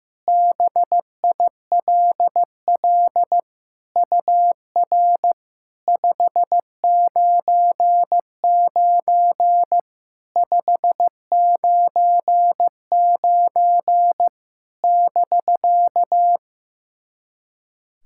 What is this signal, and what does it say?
Morse code, heard as a keyed tone, about 16 wpm, 690 Hz. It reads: BILL UR 599 599 <BK>
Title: SOTA QSO Practice 15wpm